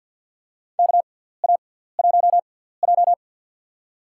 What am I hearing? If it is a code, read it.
Morse code, heard as a keyed tone, about 49 wpm, spaced as for 17 wpm, 700 Hz. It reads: KA1J
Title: USA Call Signs - 6x Character Spacing - 50wpm